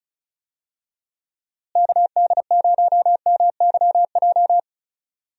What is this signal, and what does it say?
Morse code, heard as a keyed tone, about 36 wpm, 700 Hz. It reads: KD0MYJ
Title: USA Call Signs - 6 Characters - Repeated 2x - 35wpm